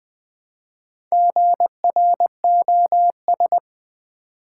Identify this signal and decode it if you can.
Morse code, heard as a keyed tone, about 20 wpm, 700 Hz. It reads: GROS
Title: Build A Word - 1 Letter At A Time - 5-Letter Words - 20wpm